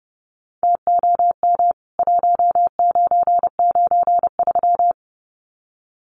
Morse code, heard as a keyed tone, about 30 wpm, 700 Hz. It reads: TOM 1993
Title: CWT Exchange 30wpm